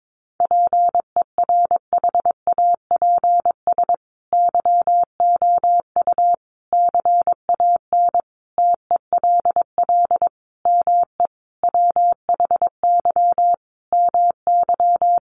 Morse code, heard as a keyed tone, about 22 wpm, 700 Hz. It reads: PERHAPS YOU CAN TELL ME WHY MY
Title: Sentences from Top 1200 Words 22wpm